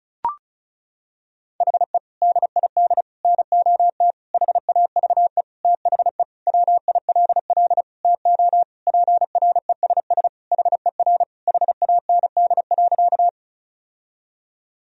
Morse code, heard as a keyed tone, about 34 wpm, 700 Hz. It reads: HE DID NOT HAVE THE WILL TO PRESS HER HAND.
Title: Sentences from Top 1000 Words - Review Part 1 - 35wpm